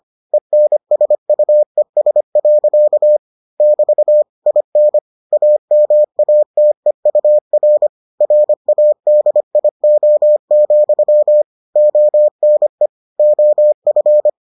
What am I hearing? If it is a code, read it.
Morse code, heard as a keyed tone, about 25 wpm, 600 Hz. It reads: ENSUES. <BT> IN AMATEUR RADIO, ONE OF